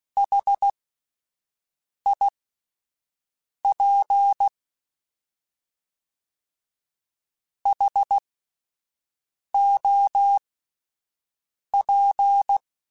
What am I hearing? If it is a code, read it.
Morse code, heard as a keyed tone, about 16 wpm, spaced as for 5 wpm, 800 Hz. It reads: HIP HOP
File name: H Prefixes / H mp3s